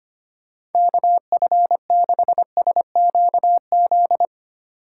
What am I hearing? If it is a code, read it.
Morse code, heard as a keyed tone, about 25 wpm, 700 Hz. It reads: KF6SQZ
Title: USA Call Signs - 6 Characters - 4x Character Spacing - 25wpm